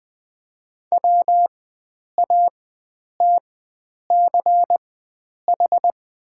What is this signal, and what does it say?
Morse code, heard as a keyed tone, about 20 wpm, spaced as for 9 wpm, 700 Hz. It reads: WATCH
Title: 4x Character Spacing - Top 500 Words - 20wpm